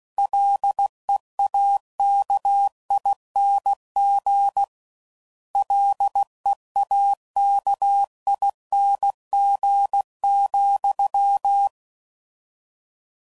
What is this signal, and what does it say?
Morse code, heard as a keyed tone, about 16 wpm, 800 Hz. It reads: LEAKING LEAKING,